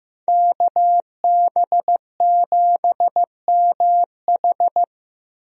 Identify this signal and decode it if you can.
Morse code, heard as a keyed tone, about 15 wpm, 700 Hz. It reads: KB7MH
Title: USA Call Signs - 5 Characters - Repeated 3x - 15wpm